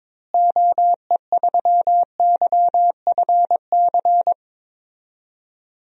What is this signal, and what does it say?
Morse code, heard as a keyed tone, about 22 wpm, 700 Hz. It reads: OE3YFC
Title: DXCC-100 Call Signs - 6 Characters - Repeated 2x - 22wpm